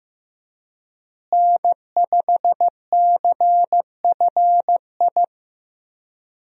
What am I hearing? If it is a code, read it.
Morse code, heard as a keyed tone, about 16 wpm, 700 Hz. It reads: N5CFI
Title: USA Call Signs - 5 Characters - Repeated 3x - 15wpm